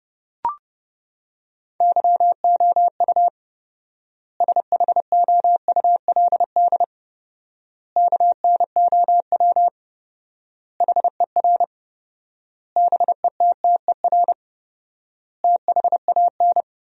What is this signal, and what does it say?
Morse code, heard as a keyed tone, about 30 wpm, 700 Hz. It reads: YOU SHOULD KNOW HER BETTER THAN
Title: 4x Word-Spacing - Sentences from Top 400 Words - 30wpm